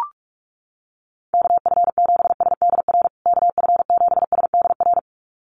Morse code, heard as a keyed tone, about 44 wpm, 700 Hz. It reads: KF7SDR KF7SDR
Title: USA Call Signs Repeated 2x 45wpm